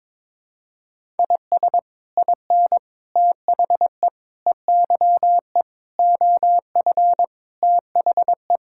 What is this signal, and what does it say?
Morse code, heard as a keyed tone, about 22 wpm, 700 Hz. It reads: IS IN THE EYE OF THE
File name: practice / 22wpm / Sets of 7 Words - 22wpm